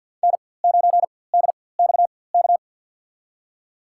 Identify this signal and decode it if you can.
Morse code, heard as a keyed tone, about 51 wpm, spaced as for 23 wpm, 690 Hz. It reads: N9DXK
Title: USA Call Signs - 5 Characters - 4x Character Spacing - 50wpm